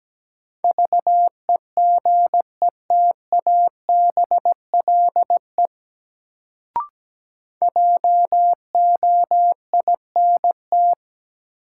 Morse code, heard as a keyed tone, about 17 wpm, 700 Hz. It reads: VEGETABLE JOINT
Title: Sentences from Top 1300 Words - New Word Drill - 17wpm